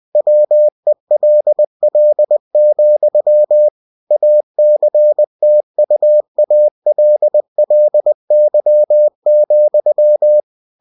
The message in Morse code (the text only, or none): WELL, ACTUALLY,